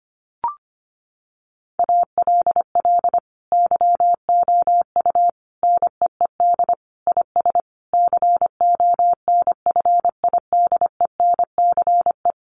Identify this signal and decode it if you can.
Morse code, heard as a keyed tone, about 25 wpm, 700 Hz. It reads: ALL YOU NEED IS CONFIDENCE
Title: Sentences from Top 1500 Words 25wpm